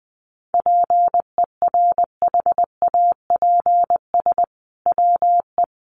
Morse code, heard as a keyed tone, about 20 wpm, 700 Hz. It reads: PERHAPS WE
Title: Sentences from Top 1500 Words 20wpm